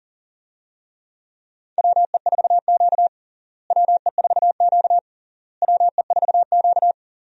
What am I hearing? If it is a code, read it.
Morse code, heard as a keyed tone, about 40 wpm, 700 Hz. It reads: WE4Q WE4Q WE4Q
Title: USA Call Signs - 4 Characters - Repeated 3x - 40wpm